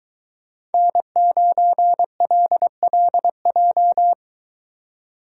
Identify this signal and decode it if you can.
Morse code, heard as a keyed tone, about 21 wpm, 700 Hz. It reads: N9LLJ